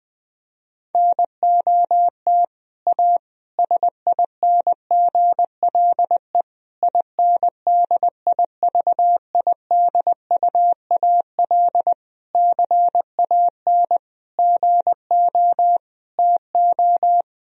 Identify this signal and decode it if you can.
Morse code, heard as a keyed tone, about 20 wpm, 700 Hz. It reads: NOT A SINGLE INDIVIDUAL CAN GO TO
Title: Sentences from Top 1000 Words - Review Part 4 - 20wpm